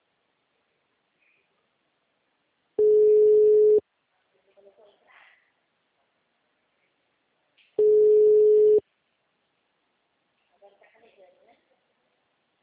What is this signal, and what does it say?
Ringback tone (Europe, 425 Hz), with speech.
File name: out- -2033- - - .348